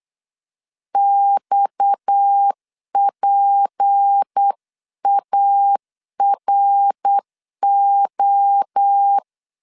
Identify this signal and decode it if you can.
Morse code, heard as a keyed tone, about 8 wpm, 800 Hz. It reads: XPARO
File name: LEZIONI / 6 PKX0 / 6 PKX 8wpm 3041